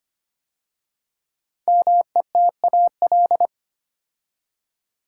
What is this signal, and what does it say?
Morse code, heard as a keyed tone, about 25 wpm, 700 Hz. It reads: METAL
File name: 5 Letter Words - First 3 Letters Given - 25wpm